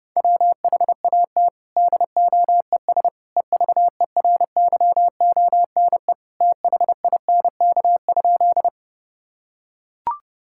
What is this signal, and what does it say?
Morse code, heard as a keyed tone, about 28 wpm, 700 Hz. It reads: WHAT DOES EVERYONE THINK?
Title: Sentences from Top 1000 Words - Review Part 5 - 30wpm